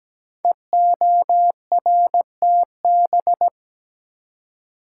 Morse code, heard as a keyed tone, about 17 wpm, 700 Hz. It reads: EORTB
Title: Code Groups - 5 Letter:Numbers - 17wpm